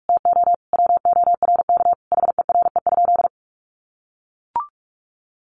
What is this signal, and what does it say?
Morse code, heard as a keyed tone, about 44 wpm, 700 Hz. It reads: TO WORK HERE?
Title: Sentences from Top 1200 Words 45wpm